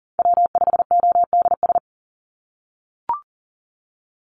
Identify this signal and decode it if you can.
Morse code, heard as a keyed tone, about 40 wpm, 700 Hz. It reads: W5ODS
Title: NA SOTA Chaser Callsigns 40wpm